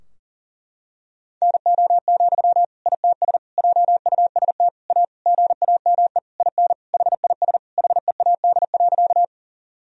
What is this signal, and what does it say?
Morse code, heard as a keyed tone, about 40 wpm, 700 Hz. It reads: NO, ITS JUST A GAME IN HIS HEAD.
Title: Sentences from Top 1000 Words - Review Part 3 - 40wpm